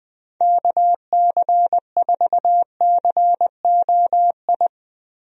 Morse code, heard as a keyed tone, about 20 wpm, 700 Hz. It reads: KC4COI
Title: USA Call Signs - 6 Characters - Repeated 2x - 20wpm